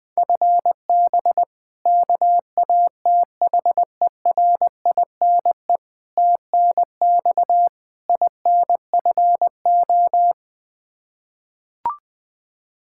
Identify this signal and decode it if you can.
Morse code, heard as a keyed tone, about 21 wpm, 700 Hz. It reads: FB KATHERINE TNX INFO
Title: Practice QSO - Exchange 3 - More Info - 20wpm